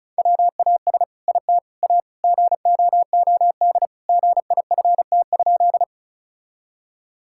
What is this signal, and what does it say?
Morse code, heard as a keyed tone, about 36 wpm, 700 Hz. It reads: WAS IT A GOOD GIFT?
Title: Sentences from Top 1000 Words - Review Part 6 - 35wpm